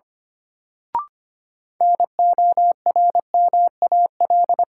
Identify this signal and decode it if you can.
Morse code, heard as a keyed tone, about 25 wpm, 700 Hz. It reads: NORMAL
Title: Sentences from Top 1200 Words - New Word Drill - 25wpm